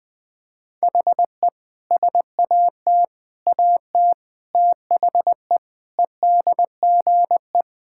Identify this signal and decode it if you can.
Morse code, heard as a keyed tone, about 20 wpm, 700 Hz. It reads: HE SAT AT THE EDGE
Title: Sentences with 5-Word Phrases - Spoken Beginning - 20wpm